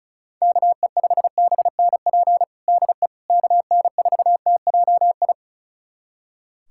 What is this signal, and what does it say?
Morse code, heard as a keyed tone, about 36 wpm, 700 Hz. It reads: KE5BNP DE KN4TJI
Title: USA Callsign DE Callsign 35wpm